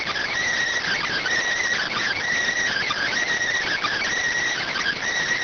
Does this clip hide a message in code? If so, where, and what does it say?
Yes: Emergency Alert System (SAME) tones from 0.8 to 5.0 s, heard as harsh buzzing bursts.